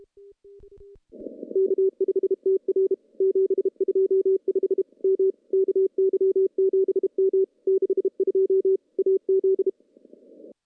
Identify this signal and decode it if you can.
Morse code, heard as a keyed tone, about 32 wpm, 380 Hz. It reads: K5TR 725 M KY7M 62 AZ
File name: CW-SS-k5tr